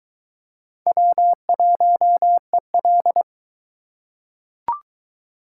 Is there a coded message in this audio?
Yes: Morse code from 0.9 to 3.2 s, heard as a keyed tone, about 22 wpm, 700 Hz. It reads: W1EL